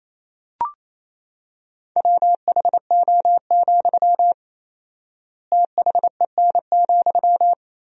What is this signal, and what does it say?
Morse code, heard as a keyed tone, about 28 wpm, 700 Hz. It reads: WHO, THEN,
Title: 4x Word-Spacing - Sentences from Top 100 Words - 28wpm